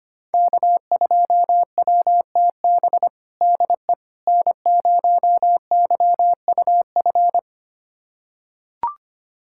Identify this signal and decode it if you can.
Morse code, heard as a keyed tone, about 25 wpm, 700 Hz. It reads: K2WTB DE N0YUF
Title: USA Callsign DE Callsign - 5 Characters - 25wpm